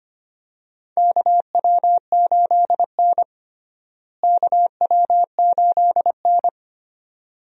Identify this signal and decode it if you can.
Morse code, heard as a keyed tone, about 25 wpm, 700 Hz. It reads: KW8N KW8N
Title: USA Call Signs - 4 Characters - Repeated 2x - 25wpm